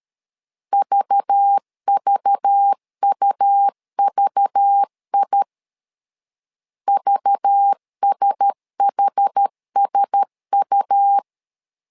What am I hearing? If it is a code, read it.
Morse code, heard as a keyed tone, about 13 wpm, 800 Hz. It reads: VVUVI VSHSU
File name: LEZIONI / 5 LBVC / 5 LBV 12wpm 1855